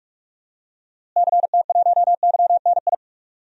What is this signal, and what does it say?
Morse code, heard as a keyed tone, about 45 wpm, 700 Hz. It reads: CT1YNI
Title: DXCC-100 Call Signs - 6 Characters - Repeated 2x - 45wpm